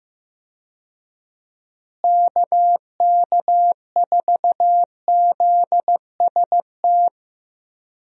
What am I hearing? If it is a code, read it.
Morse code, heard as a keyed tone, about 15 wpm, 700 Hz. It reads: KK4ZST
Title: USA Call Signs - 6 Characters - Repeated 3x - 15wpm